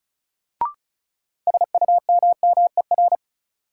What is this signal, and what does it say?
Morse code, heard as a keyed tone, about 36 wpm, 700 Hz. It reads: SUMMER
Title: Sentences from Top 1000 Words - Review Part 6 - New Word Drill - 35wpm